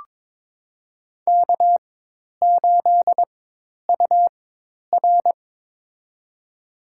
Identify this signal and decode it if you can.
Morse code, heard as a keyed tone, about 22 wpm, spaced as for 10 wpm, 700 Hz. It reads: K8UR
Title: USA Call Signs - 4x Character Spacing - 22wpm